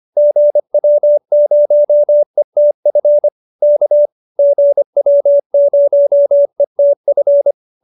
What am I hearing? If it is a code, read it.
Morse code, heard as a keyed tone, about 25 wpm, 590 Hz. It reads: GW0ETF K GW0ETF